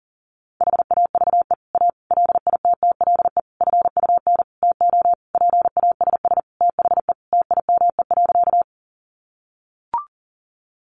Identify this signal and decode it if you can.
Morse code, heard as a keyed tone, about 40 wpm, 700 Hz. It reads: HAVE A LITTLE FUN TO PASS THE TIME.
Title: Sentences from Top 700 Words 40wpm